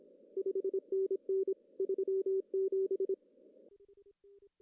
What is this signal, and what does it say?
Morse code, heard as a keyed tone, about 26 wpm, 380 Hz. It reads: 5NN 37
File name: CW / cqww-cw-k5tr